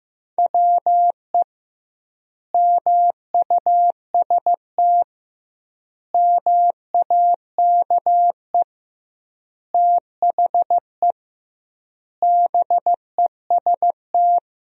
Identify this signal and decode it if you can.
Morse code, heard as a keyed tone, about 16 wpm, 700 Hz. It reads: WE MUST MAKE THE BEST
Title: 2x Word-Spacing - Sentences from Top 400 Words - 15wpm